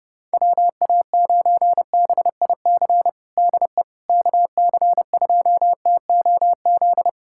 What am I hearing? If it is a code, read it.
Morse code, heard as a keyed tone, about 30 wpm, 700 Hz. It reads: WA9BIC DE KC2TOZ
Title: USA Callsign DE Callsign 30wpm